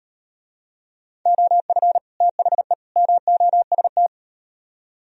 Morse code, heard as a keyed tone, about 38 wpm, 700 Hz. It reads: OF THE MOST